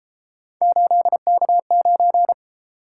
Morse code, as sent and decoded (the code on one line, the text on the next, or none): ---.. -.- ----.
8K9